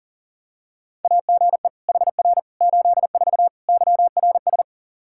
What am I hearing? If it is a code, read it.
Morse code, heard as a keyed tone, about 37 wpm, 700 Hz. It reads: AGE HR 84 YRS
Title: Practice QSO - Exchange 3 - More Info - 40wpm